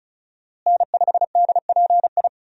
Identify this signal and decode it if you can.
Morse code, heard as a keyed tone, about 34 wpm, 700 Hz. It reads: N5DPI
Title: USA Call Signs - 5 Characters - Repeated 3x - 35wpm